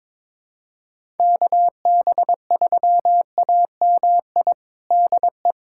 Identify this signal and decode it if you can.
Morse code, heard as a keyed tone, about 22 wpm, 700 Hz. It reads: KB3AMI DE
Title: USA Callsign DE Callsign - 6 Characters - 22wpm